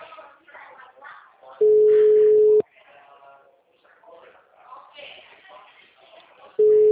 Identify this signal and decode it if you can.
Ringback tone (Europe, 425 Hz), with speech.